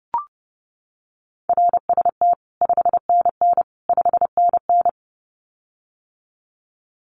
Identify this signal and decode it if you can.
Morse code, heard as a keyed tone, about 30 wpm, 700 Hz. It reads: RST 5NN 5NN
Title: Common QSO Phrases Encore 30wpm